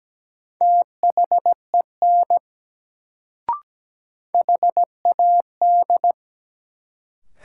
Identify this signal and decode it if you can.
Morse code, heard as a keyed tone, about 17 wpm, 700 Hz. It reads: THEN HAD
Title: Sentences from Top 100 Words - New Word Drill - 17wpm